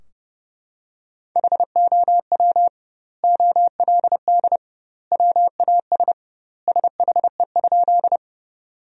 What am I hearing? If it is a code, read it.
Morse code, heard as a keyed tone, about 32 wpm, 700 Hz. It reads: HOW OLD WAS SHE?
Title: 2x Word-Spacing - Sentences from Top 200 Words - 30wpm